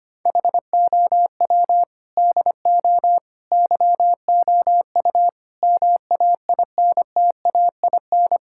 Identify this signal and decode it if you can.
Morse code, heard as a keyed tone, about 25 wpm, 700 Hz. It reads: HOW DO YOU MAINTAIN
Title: Sentences from Top 1000 Words - Review Part 6 - 25wpm